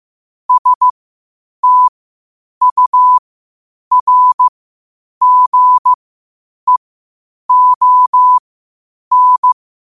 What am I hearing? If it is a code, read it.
Morse code, heard as a keyed tone, about 15 wpm, spaced as for 8 wpm, 1000 Hz. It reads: STURGEON
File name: qotdfiles / 8wpm